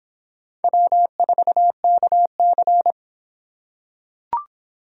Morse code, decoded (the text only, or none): W4KC